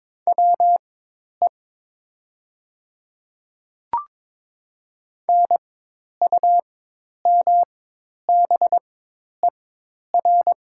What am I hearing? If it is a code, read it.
Morse code, heard as a keyed tone, about 22 wpm, spaced as for 10 wpm, 700 Hz. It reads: WE NUMBER
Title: 4x Character Spacing - Top 200 Words - 22wpm